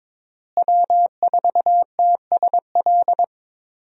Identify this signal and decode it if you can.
Morse code, heard as a keyed tone, about 22 wpm, 700 Hz. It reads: W4TSL
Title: USA Call Signs - 5 Characters - Repeated 2x - 22wpm